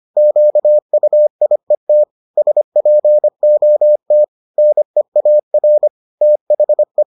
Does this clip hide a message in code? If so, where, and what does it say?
Yes: Morse code from 0.2 to 7.0 s, heard as a keyed tone, about 25 wpm, 600 Hz. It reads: QUIET SPOT NEAR THE